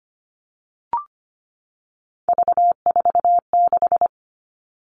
Morse code, heard as a keyed tone, about 25 wpm, 700 Hz. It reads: V46
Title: All Cut Numbers - 3-Digit 25wpm